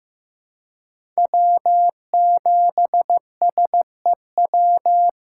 Morse code, heard as a keyed tone, about 15 wpm, 700 Hz. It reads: W7SEW
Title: USA Call Signs - 5 Characters - 2x Character Spacing - 15wpm